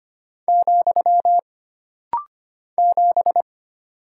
Morse code, heard as a keyed tone, about 25 wpm, 700 Hz. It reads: , 7
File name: Single Character (Letter:Number:Punctuation:Prosign) - Rapid-Fire - 25wpm